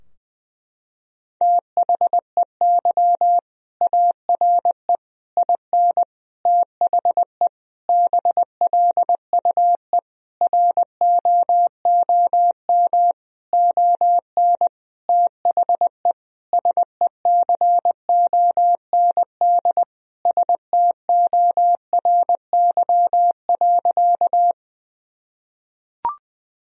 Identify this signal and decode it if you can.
Morse code, heard as a keyed tone, about 20 wpm, 700 Hz. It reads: THEY ARE IN THE BLUE ROOM ON THE SECOND STORY.
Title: Sentences from Top 1000 Words - Review Part 3 - 20wpm